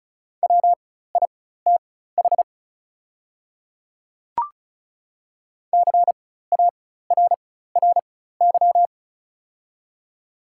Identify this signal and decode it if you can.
Morse code, heard as a keyed tone, about 36 wpm, spaced as for 16 wpm, 690 Hz. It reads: WITH CARRY
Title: 4x Character Spacing - Top 300 Words - 35wpm